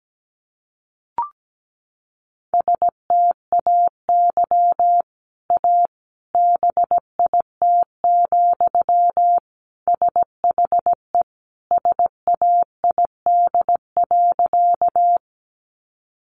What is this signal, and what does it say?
Morse code, heard as a keyed tone, about 17 wpm, 700 Hz. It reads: STAY A BIT, SHE SAID.
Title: Sentences from Top 900 Words 17wpm